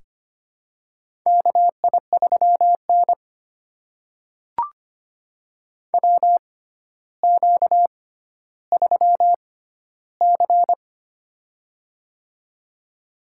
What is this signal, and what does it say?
Morse code, heard as a keyed tone, about 25 wpm, spaced as for 13 wpm, 700 Hz. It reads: KI3N WQ3C
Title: USA Call Signs - 4 Characters - 6x Character Spacing - 25wpm